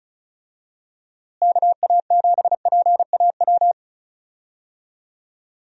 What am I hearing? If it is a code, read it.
Morse code, heard as a keyed tone, about 36 wpm, 700 Hz. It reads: KA7PAW
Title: USA Call Signs - 6 Characters - Repeated 3x - 35wpm